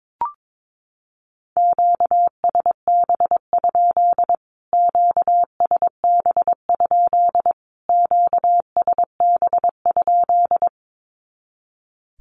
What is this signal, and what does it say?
Morse code, heard as a keyed tone, about 22 wpm, 700 Hz. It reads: QSB? QSB? QSB?
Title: Q-Codes Repeated 3x 22wpm